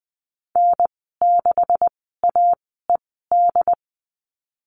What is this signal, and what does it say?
Morse code, heard as a keyed tone, about 20 wpm, spaced as for 15 wpm, 700 Hz. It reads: N6AED
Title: 219 Lesson 35g - Number 0 - Call Signs 20wpm